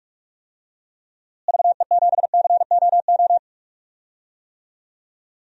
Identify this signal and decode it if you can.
Morse code, heard as a keyed tone, about 42 wpm, 700 Hz. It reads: VE7COO
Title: DXCC-100 Call Signs - 6 Characters - Repeated 3x - 45wpm